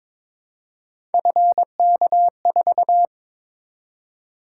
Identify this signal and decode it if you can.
Morse code, heard as a keyed tone, about 23 wpm, 700 Hz. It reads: FK4
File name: practice / 22wpm / Three Letter-Number Combinations - 22wpm